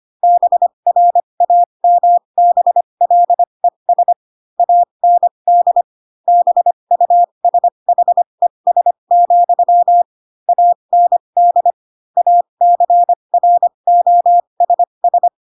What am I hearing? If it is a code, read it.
Morse code, heard as a keyed tone, about 25 wpm, spaced as for 22 wpm, 700 Hz. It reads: BRAMBLES AND BUSHES, AND ACROSS
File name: Pinocchio / Pinocchio ch4 18 25 ik1hge